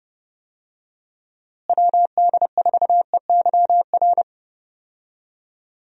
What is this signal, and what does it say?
Morse code, heard as a keyed tone, about 29 wpm, 700 Hz. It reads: WD4EYR
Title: USA Call Signs - 6 Characters - Repeated 3x - 30wpm